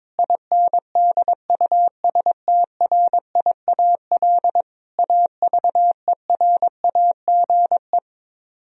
Morse code, heard as a keyed tone, about 22 wpm, 700 Hz. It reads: INDUSTRIAL AVERAGE
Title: Speed-Racing - Sets of 5 Words 22wpm